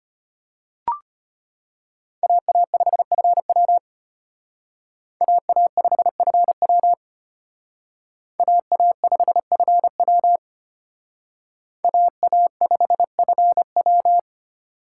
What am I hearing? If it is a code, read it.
Morse code, heard as a keyed tone, about 32 wpm, 700 Hz. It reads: AA5FW AA5FW AA5FW AA5FW